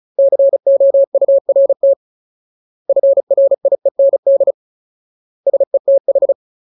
Morse code, heard as a keyed tone, about 34 wpm, 550 Hz. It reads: COURT FRIEND SETH
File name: Ultimate Word List / 35WPM Ultimate